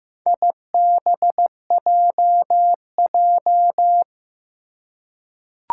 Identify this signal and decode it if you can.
Morse code, heard as a keyed tone, about 15 wpm, 700 Hz. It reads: IBJJ
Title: Four Challenging Letter-Number Combinations 15wpm